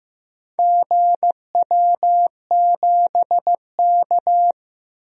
Morse code, heard as a keyed tone, about 15 wpm, 700 Hz. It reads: GW7K
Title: DXCC-100 Call Signs - 4 Characters - 15wpm